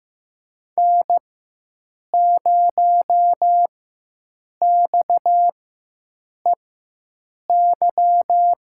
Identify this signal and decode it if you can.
Morse code, heard as a keyed tone, about 15 wpm, spaced as for 7 wpm, 700 Hz. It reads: N0XEY
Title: USA Call Signs - 4x Character Spacing - 15wpm